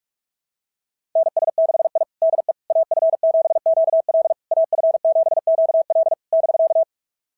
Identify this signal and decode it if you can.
Morse code, heard as a keyed tone, about 46 wpm, 690 Hz. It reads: NS6I DE AF7QL AF7QL <BK>
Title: SOTA QSO Practice 45wpm